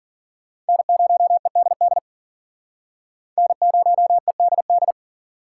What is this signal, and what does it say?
Morse code, heard as a keyed tone, about 51 wpm, 700 Hz. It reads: N0EDD N0EDD